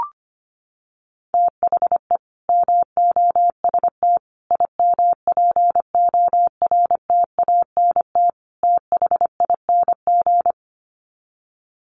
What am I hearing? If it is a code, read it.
Morse code, heard as a keyed tone, about 25 wpm, 700 Hz. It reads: THE MOST IMPORTANT THING
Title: Sets of 4 Words 25wpm